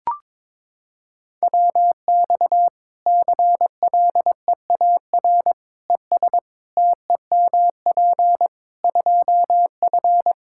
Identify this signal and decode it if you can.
Morse code, heard as a keyed tone, about 22 wpm, 700 Hz. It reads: WX CLEAR ES TEMP 2F
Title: Practice QSO - Exchange 2 - Info - 22wpm